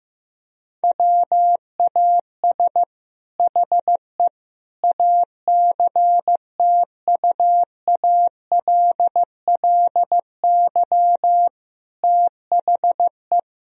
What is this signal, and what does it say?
Morse code, heard as a keyed tone, about 15 wpm, 700 Hz. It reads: WAS HE ACTUALLY THE
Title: Sentences from Top 1000 Words - Review Part 4 - 15wpm